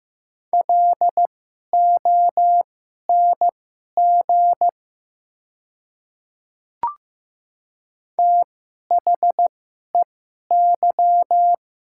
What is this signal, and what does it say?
Morse code, heard as a keyed tone, about 15 wpm, spaced as for 11 wpm, 700 Hz. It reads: LONG THEY
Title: 2x Character Spacing - Top 100 Words - 15wpm